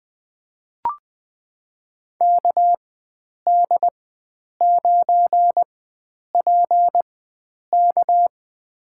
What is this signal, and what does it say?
Morse code, heard as a keyed tone, about 20 wpm, spaced as for 9 wpm, 700 Hz. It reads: KD9PK
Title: USA Call Signs - 5 Characters - 4x Character Spacing - 20wpm